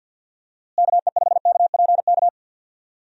Morse code, heard as a keyed tone, about 49 wpm, 700 Hz. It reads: KE5KPK
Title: USA Call Signs - 6 Characters - Repeated 2x - 50wpm